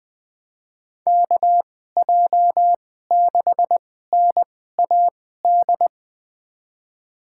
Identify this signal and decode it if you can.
Morse code, heard as a keyed tone, about 20 wpm, spaced as for 15 wpm, 700 Hz. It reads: KJ6NAD
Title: USA Call Signs - 6 Characters - 2x Character Spacing - 20wpm